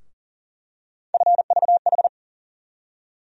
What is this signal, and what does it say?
Morse code, heard as a keyed tone, about 40 wpm, 700 Hz. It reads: FVH
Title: USA Call Sign Postfixes - 3 Letters 40wpm